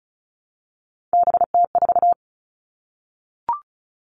Morse code, heard as a keyed tone, about 36 wpm, 700 Hz. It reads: BT4
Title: Three Challenging Letter-Number Combinations 35wpm